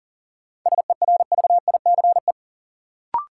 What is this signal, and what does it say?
Morse code, heard as a keyed tone, about 40 wpm, 700 Hz. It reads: SERVICE
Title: Sentences from Top 700 Words - New Word Drill - 40wpm